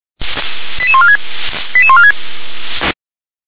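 Five-tone selective call (ZVEI), heard as quick runs of five tones, tones of about 70 ms. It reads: 88146 88146
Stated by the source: ZVEI five-tone selective call 88083, received by SDR.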